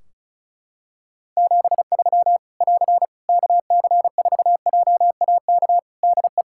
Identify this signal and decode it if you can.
Morse code, heard as a keyed tone, about 34 wpm, 700 Hz. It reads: 73 <AR> KC4JAK DE
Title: Practice QSO - Exchange 4 - Ending - 35wpm